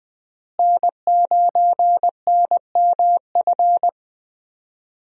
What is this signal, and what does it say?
Morse code, heard as a keyed tone, about 20 wpm, 700 Hz. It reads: N9NMF
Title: USA Call Signs - 5 Characters - Repeated 2x - 20wpm